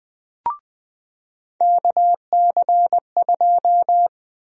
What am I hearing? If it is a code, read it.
Morse code, heard as a keyed tone, about 20 wpm, 700 Hz. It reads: KC2
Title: USA Call Sign Prefixes - 20wpm